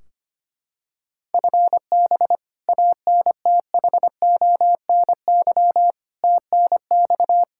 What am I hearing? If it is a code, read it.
Morse code, heard as a keyed tone, about 25 wpm, 700 Hz. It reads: FB ANTHONY TNX
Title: K1USN SST QSO Practice 25wpm